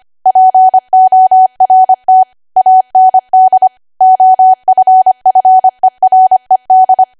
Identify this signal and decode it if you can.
Morse code, heard as a keyed tone, about 25 wpm, 750 Hz. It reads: PORT AND OFFERED
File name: Archive / 25 WPM / 25WPM